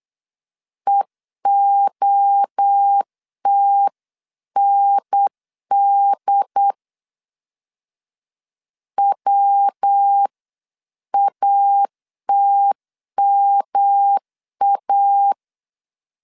Morse code, heard as a keyed tone, about 8 wpm, 800 Hz. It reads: EOTND WATMA